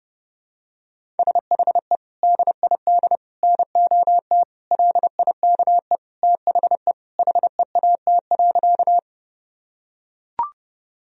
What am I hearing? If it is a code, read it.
Morse code, heard as a keyed tone, about 30 wpm, 700 Hz. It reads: SHE DID NOT LIKE THE HEAT.
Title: Sentences from Top 1000 Words - Review Part 3 - 30wpm